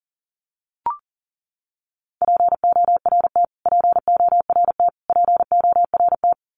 Morse code, heard as a keyed tone, about 40 wpm, 700 Hz. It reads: PORT PORT PORT
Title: Top 200 Words Repeated 40wpm